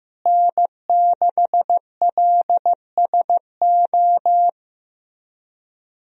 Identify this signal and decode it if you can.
Morse code, heard as a keyed tone, about 16 wpm, 700 Hz. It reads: N6LSO
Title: USA Call Signs - 5 Characters - Repeated 2x - 15wpm